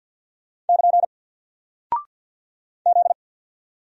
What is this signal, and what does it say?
Morse code, heard as a keyed tone, about 48 wpm, 700 Hz. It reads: <KN> Z
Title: Single Character (Letter:Number:Punctuation:Prosign) Alt 50wpm